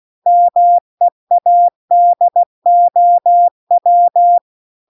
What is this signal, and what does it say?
Morse code, heard as a keyed tone, about 16 wpm, 700 Hz. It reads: MEADOW